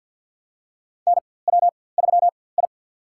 Morse code, heard as a keyed tone, about 45 wpm, spaced as for 23 wpm, 700 Hz. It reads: NW3I
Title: USA Call Signs - 4x Character Spacing - 50wpm